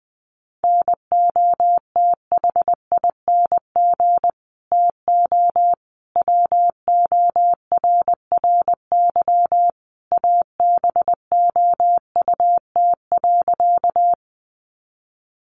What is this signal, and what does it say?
Morse code, heard as a keyed tone, about 20 wpm, 700 Hz. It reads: NOTHING TO WORRY ABOUT.
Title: Sentences from Top 1100 Words 20wpm